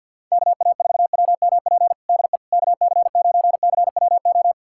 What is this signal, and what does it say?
Morse code, heard as a keyed tone, about 53 wpm, 700 Hz. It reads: KA4WMP DE KC9CWO
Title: USA Callsign DE Callsign - 6 Characters - 50wpm